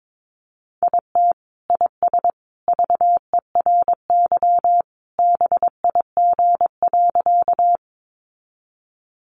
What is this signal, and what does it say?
Morse code, heard as a keyed tone, about 22 wpm, 700 Hz. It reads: IT IS VERY BIG.
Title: Easy Phrases 22wpm